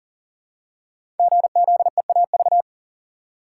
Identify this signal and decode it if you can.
Morse code, heard as a keyed tone, about 41 wpm, 700 Hz. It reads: GZEAV
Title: Code Groups - 5 Letter:Numbers - 40wpm